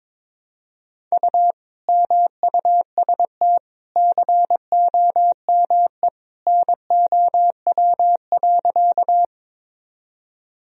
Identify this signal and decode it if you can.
Morse code, heard as a keyed tone, about 22 wpm, 700 Hz. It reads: U MUST COME NOW.
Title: Sentences from Top 200 Words and Abbreviations 22wpm